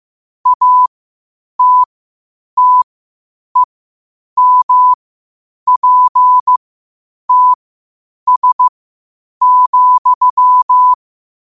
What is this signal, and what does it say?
Morse code, heard as a keyed tone, about 15 wpm, spaced as for 8 wpm, 1000 Hz. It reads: ATTEMPTS,
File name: qotdfiles / 8wpm